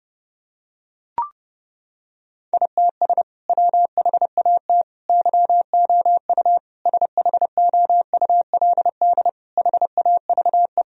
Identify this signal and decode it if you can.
Morse code, heard as a keyed tone, about 30 wpm, 700 Hz. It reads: ITS WHAT YOU SHOULD HAVE
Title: Sentences from Top 1000 Words - Review Part 3 - 30wpm